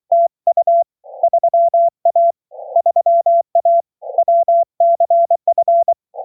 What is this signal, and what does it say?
Morse code, heard as a keyed tone, about 24 wpm, 690 Hz. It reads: TU 3A 3A WCF